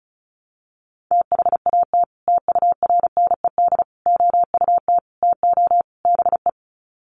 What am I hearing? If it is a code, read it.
Morse code, heard as a keyed tone, about 34 wpm, 700 Hz. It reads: THAT TURNED OUT TO BE
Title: Sets of 5 Words Encore2 35wpm